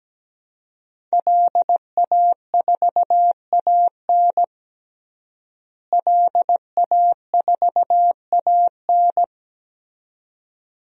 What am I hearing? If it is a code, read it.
Morse code, heard as a keyed tone, about 17 wpm, 700 Hz. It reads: LA4AN LA4AN
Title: DXCC-100 Call Signs - 5 Characters - Repeated 2x - 17wpm